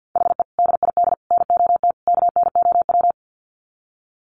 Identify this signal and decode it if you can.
Morse code, heard as a keyed tone, about 50 wpm, 700 Hz. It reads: HE DID NOT KNOW
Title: Sets of 4 Words 50wpm